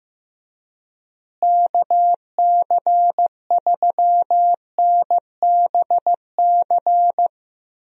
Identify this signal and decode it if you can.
Morse code, heard as a keyed tone, about 15 wpm, 700 Hz. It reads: KC3NBC
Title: USA Call Signs - 6 Characters - Repeated 2x - 15wpm